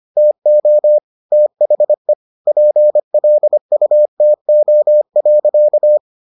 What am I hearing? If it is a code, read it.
Morse code, heard as a keyed tone, about 25 wpm, 600 Hz. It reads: TO THE PLUTO.